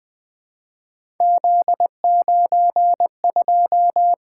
Morse code, heard as a keyed tone, about 20 wpm, 700 Hz. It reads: Z92
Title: Three Letter-Number Combinations - 20wpm